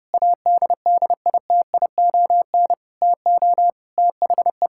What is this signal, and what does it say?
Morse code, heard as a keyed tone, about 30 wpm, 700 Hz. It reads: ADDITION TO THE
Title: Sentences from Top 1000 Words - Review Part 6 - 30wpm